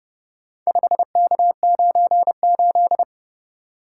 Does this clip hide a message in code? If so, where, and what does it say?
Yes: Morse code from 0.7 to 3.0 s, heard as a keyed tone, about 29 wpm, 700 Hz. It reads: 5K98